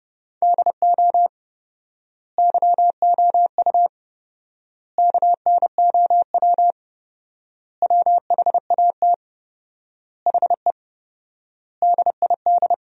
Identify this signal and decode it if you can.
Morse code, heard as a keyed tone, about 32 wpm, 700 Hz. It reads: DO YOU KNOW WHAT HE DID
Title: 4x Word-Spacing - Sentences from Top 300 Words - 30wpm